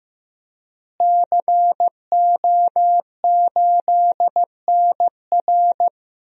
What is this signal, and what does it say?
Morse code, heard as a keyed tone, about 15 wpm, 700 Hz. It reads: CO8NR
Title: DXCC-100 Call Signs - 5 Characters - Repeated 3x - 15wpm